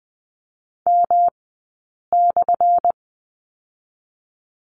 Morse code, heard as a keyed tone, about 20 wpm, spaced as for 8 wpm, 690 Hz. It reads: M/
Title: 217 Lesson 35d - Number 0 - Two Characters 20wpm